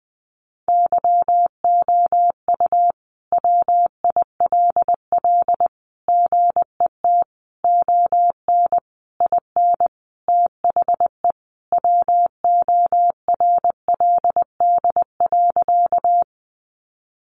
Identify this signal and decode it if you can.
Morse code, heard as a keyed tone, about 20 wpm, 700 Hz. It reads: YOU WILL GET ON IN THE WORLD.
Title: Sentences from Top 200 Words 20wpm